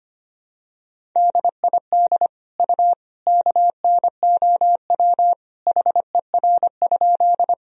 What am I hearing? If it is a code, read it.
Morse code, heard as a keyed tone, about 25 wpm, 700 Hz. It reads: DID U KNOW HER?
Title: Sentences from Top 100 Words and Abbreviations 25wpm